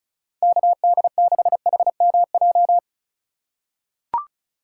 Morse code, heard as a keyed tone, about 36 wpm, 700 Hz. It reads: KD6HMJ
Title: USA Call Signs - 6 Characters - 2x Character Spacing - 35wpm